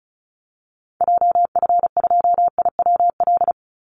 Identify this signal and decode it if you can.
Morse code, heard as a keyed tone, about 34 wpm, 700 Hz. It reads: JF2IWL
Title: Call Signs Repeated 2x - 35wpm